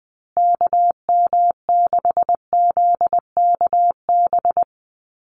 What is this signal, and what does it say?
Morse code, heard as a keyed tone, about 20 wpm, 700 Hz. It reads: KM6ZKB
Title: USA Call Signs Repeated 2x 20wpm